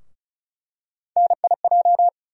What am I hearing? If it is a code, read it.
Morse code, heard as a keyed tone, about 36 wpm, 700 Hz. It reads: NIJ
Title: USA Call Sign Postfixes - 3 Letters 35wpm